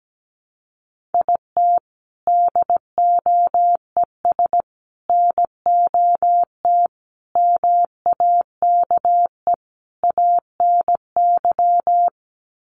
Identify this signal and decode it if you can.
Morse code, heard as a keyed tone, about 17 wpm, 700 Hz. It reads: IT DOES NOT MAKE ANY
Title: Sets of 5 Words Encore2 17wpm